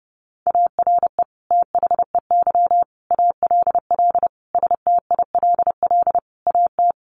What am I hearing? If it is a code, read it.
Morse code, heard as a keyed tone, about 30 wpm, 700 Hz. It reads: ARE THEY ALL STILL AT
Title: Sentences from Top 1500 Words 30wpm